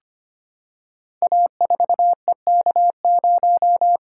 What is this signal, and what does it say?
Morse code, heard as a keyed tone, about 25 wpm, 700 Hz. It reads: A4EK0
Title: Code Groups - 5 Letter:Numbers - 25wpm